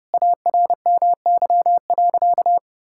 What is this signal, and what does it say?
Morse code, heard as a keyed tone, about 30 wpm, 700 Hz. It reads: ARMY.
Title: Sentences from Top 1000 Words - Review Part 5 - 30wpm